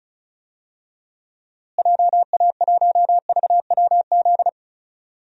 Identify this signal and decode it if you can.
Morse code, heard as a keyed tone, about 36 wpm, 700 Hz. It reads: JA1VWZ
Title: DXCC-100 Call Signs - 6 Characters - Repeated 3x - 35wpm